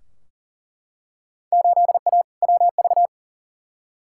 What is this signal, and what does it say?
Morse code, heard as a keyed tone, about 40 wpm, 700 Hz. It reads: 8A WV
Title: Field Day Exchanges Encore - 40wpm